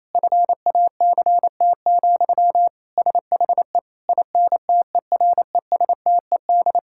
Morse code, heard as a keyed tone, about 28 wpm, 700 Hz. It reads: FACT, SHE INTERESTED
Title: Sentences from Top 1000 Words - Review Part 6 - 28wpm